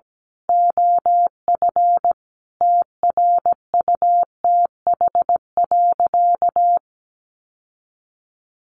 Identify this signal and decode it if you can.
Morse code, heard as a keyed tone, about 17 wpm, 700 Hz. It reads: OF TRUTH.
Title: Sentences from Top 1500 Words 17wpm